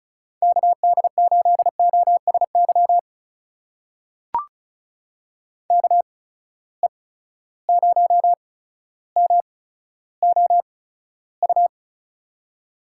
Morse code, heard as a keyed tone, about 36 wpm, spaced as for 15 wpm, 700 Hz. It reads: KD8OSY KE0MOU
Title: USA Call Signs - 6 Characters - 8x Character Spacing - 35wpm